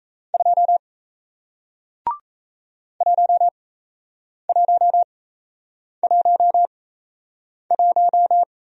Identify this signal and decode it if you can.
Morse code, heard as a keyed tone, about 38 wpm, spaced as for 8 wpm, 700 Hz. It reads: 2 1111